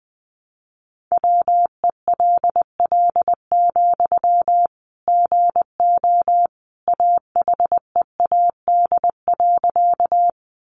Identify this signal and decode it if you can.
Morse code, heard as a keyed tone, about 20 wpm, 700 Hz. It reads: WELL, GO AHEAD.
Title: Sentences from Top 1100 Words 20wpm